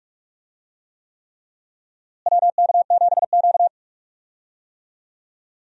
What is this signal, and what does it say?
Morse code, heard as a keyed tone, about 47 wpm, 700 Hz. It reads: WK7Q
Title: USA Call Signs - 4 Characters - Repeated 3x - 45wpm